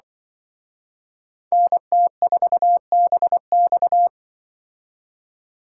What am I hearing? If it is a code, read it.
Morse code, heard as a keyed tone, about 24 wpm, 700 Hz. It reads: NT4BX
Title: Speed Racing - USA 2x2 Call Signs 20wpm